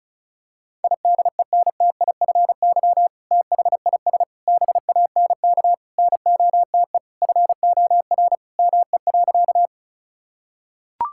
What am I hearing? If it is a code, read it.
Morse code, heard as a keyed tone, about 36 wpm, 700 Hz. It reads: IDENTIFY THIS BANK NOTE FOR ME.
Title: Sentences from Top 1000 Words - Review Part 6 - 35wpm